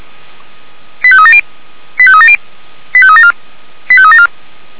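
Five-tone selective call (ZVEI), heard as quick runs of five tones, tones of about 70 ms. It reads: 85370 85370 85374 85374
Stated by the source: ZVEI five-tone selective call 89227 received by SDR, logged by Feuerwehr Orlamünde fire brigade.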